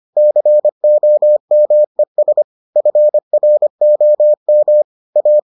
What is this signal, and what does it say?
Morse code, heard as a keyed tone, about 25 wpm, 600 Hz. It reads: COMES FROM A